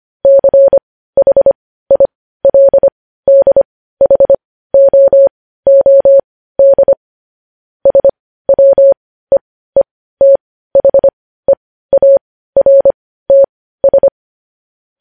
Morse code, heard as a keyed tone, about 25 wpm, spaced as for 15 wpm, 550 Hz. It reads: CHILDHOOD SWEETHEARTS
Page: news-headlines-in-morse-code-at-15-wpm